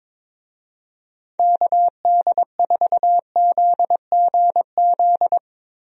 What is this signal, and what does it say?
Morse code, heard as a keyed tone, about 22 wpm, 700 Hz. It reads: KD4ZGZ
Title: USA Call Signs - 6 Characters - Repeated 3x - 22wpm